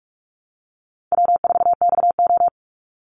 Morse code, heard as a keyed tone, about 44 wpm, 700 Hz. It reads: W4XO
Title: Easy Call Signs Repeated 2x - 45wpm